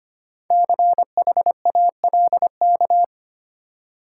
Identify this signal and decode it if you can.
Morse code, heard as a keyed tone, about 25 wpm, 700 Hz. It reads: CHALK
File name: practice / 25wpm / Easy Words with Hints - 25wpm